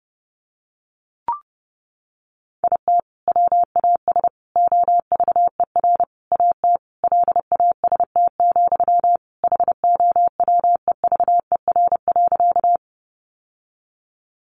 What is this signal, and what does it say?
Morse code, heard as a keyed tone, about 28 wpm, 700 Hz. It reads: IT WAS OVER AT LAST, HOWEVER.
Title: Sentences from Top 600 Words 30wpm